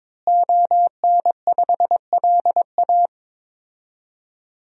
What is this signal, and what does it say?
Morse code, heard as a keyed tone, about 22 wpm, 700 Hz. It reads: ON5LA
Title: DXCC-100 Call Signs - 5 Characters - Repeated 3x - 22wpm